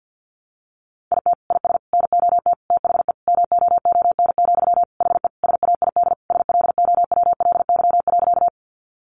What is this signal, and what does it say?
Morse code, heard as a keyed tone, about 50 wpm, 700 Hz. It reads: IT IS NOT THE MOON, HE SAID SLOWLY.
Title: Sentences from Top 1100 Words 50wpm